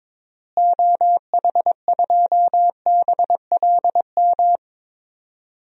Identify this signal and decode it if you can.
Morse code, heard as a keyed tone, about 22 wpm, 700 Hz. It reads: OH2BLM
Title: DXCC-100 Call Signs - 6 Characters - Repeated 2x - 22wpm